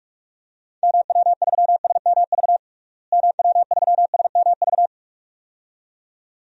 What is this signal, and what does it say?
Morse code, heard as a keyed tone, about 41 wpm, 700 Hz. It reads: MW3SMV MW3SMV
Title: DXCC-100 Call Signs - 6 Characters - Repeated 2x - 45wpm